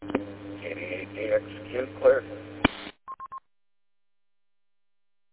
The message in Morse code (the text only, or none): none